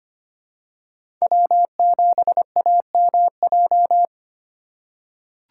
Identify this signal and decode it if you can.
Morse code, heard as a keyed tone, about 25 wpm, 700 Hz. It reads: W7AMJ
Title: USA Call Signs - 5 Characters - 4x Character Spacing - 25wpm